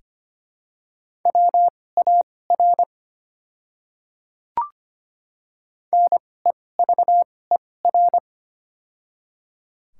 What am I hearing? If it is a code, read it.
Morse code, heard as a keyed tone, about 25 wpm, spaced as for 18 wpm, 700 Hz. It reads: WAR NEVER
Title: 2x Character Spacing - Top 500 Words - 25wpm